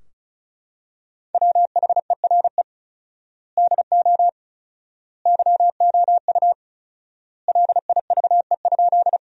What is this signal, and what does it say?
Morse code, heard as a keyed tone, about 36 wpm, 700 Hz. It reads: WHERE DO YOU LIVE?
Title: 4x Word-Spacing - Sentences from Top 200 Words - 35wpm